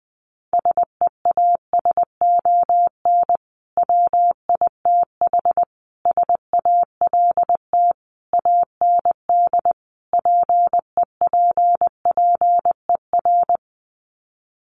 Morse code, heard as a keyed tone, about 20 wpm, 700 Hz. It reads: SEASON WITH SALT AND PEPPER
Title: Sets of 5 Words 20wpm